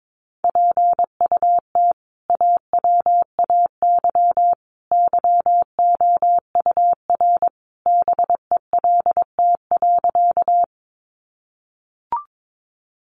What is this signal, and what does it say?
Morse code, heard as a keyed tone, about 22 wpm, 700 Hz. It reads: PUT AWAY YOUR BELT.
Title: Sentences from Top 1400 Words 22wpm